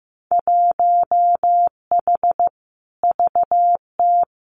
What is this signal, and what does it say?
Morse code, heard as a keyed tone, about 15 wpm, 700 Hz. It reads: 1H VT
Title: Winter Field Day Exchanges Encore 15wpm